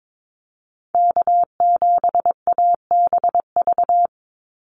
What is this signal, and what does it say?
Morse code, heard as a keyed tone, about 22 wpm, 700 Hz. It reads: K7ABV
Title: Call Signs Repeated 3x - 22wpm